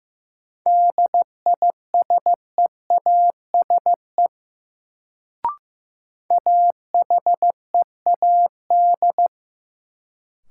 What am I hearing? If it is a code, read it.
Morse code, heard as a keyed tone, about 15 wpm, 700 Hz. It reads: DISEASE AHEAD
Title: Sentences from Top 1000 Words - Review Part 6 - New Word Drill - 15wpm